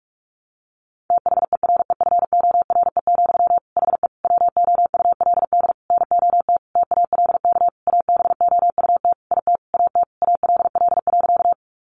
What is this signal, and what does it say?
Morse code, heard as a keyed tone, about 44 wpm, 700 Hz. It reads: THEREFORE, HE WOULD NOT TALK ABOUT IT AT ALL.
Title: Sentences from Top 800 Words 45wpm